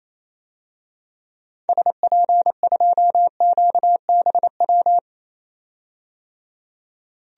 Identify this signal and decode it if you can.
Morse code, heard as a keyed tone, about 27 wpm, 700 Hz. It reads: SP2QBW
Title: DXCC-100 Call Signs - 6 Characters - 28wpm